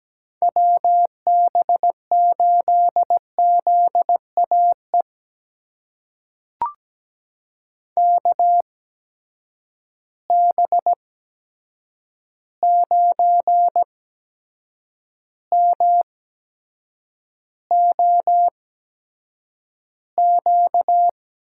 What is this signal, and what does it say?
Morse code, heard as a keyed tone, about 17 wpm, spaced as for 7 wpm, 700 Hz. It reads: WB8ZAE KB9MOQ
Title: USA Call Signs - 6 Characters - 8x Character Spacing - 17wpm